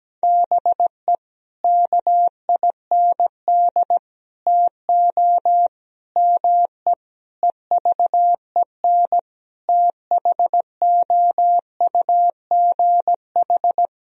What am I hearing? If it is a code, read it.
Morse code, heard as a keyed tone, about 17 wpm, 700 Hz. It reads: BE KIND TO ME EVEN THOUGH
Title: Sentences from Top 1000 Words - Review Part 2 - 17wpm